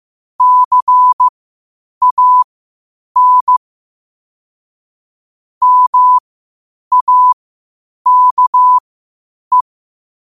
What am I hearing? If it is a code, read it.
Morse code, heard as a keyed tone, about 15 wpm, spaced as for 8 wpm, 1000 Hz. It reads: CAN MAKE
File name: qotdfiles / 8wpm